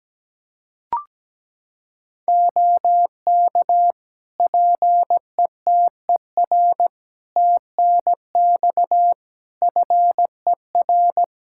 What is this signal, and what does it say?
Morse code, heard as a keyed tone, about 17 wpm, 700 Hz. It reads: OK PETER TNX FER
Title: K1USN SST QSO Practice 17wpm